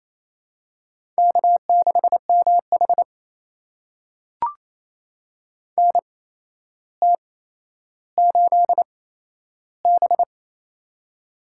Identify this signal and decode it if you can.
Morse code, heard as a keyed tone, about 27 wpm, spaced as for 12 wpm, 700 Hz. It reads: K6MH NT8B
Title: USA Call Signs - 4 Characters - 8x Character Spacing - 28wpm